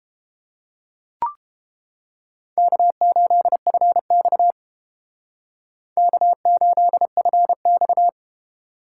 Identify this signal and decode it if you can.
Morse code, heard as a keyed tone, about 31 wpm, 700 Hz. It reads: K8FX K8FX